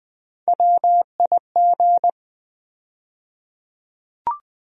Morse code, heard as a keyed tone, about 20 wpm, 700 Hz. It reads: WIG
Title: USA Call Sign Postfixes - 3 Letters 20wpm